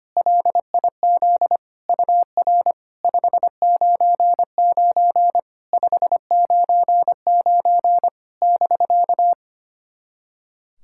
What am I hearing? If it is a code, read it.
Morse code, heard as a keyed tone, about 25 wpm, 700 Hz. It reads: LIZ UR 599 599 <BK>
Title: SOTA QSO Practice 25wpm